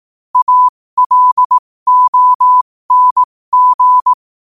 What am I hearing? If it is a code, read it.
Morse code, heard as a keyed tone, about 18 wpm, spaced as for 15 wpm, 1000 Hz. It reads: ALONG
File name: qotdfiles / 15wpm